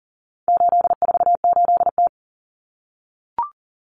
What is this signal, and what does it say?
Morse code, heard as a keyed tone, about 40 wpm, 700 Hz. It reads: 848T
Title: Common Cut Numbers - 4-Digit 40wpm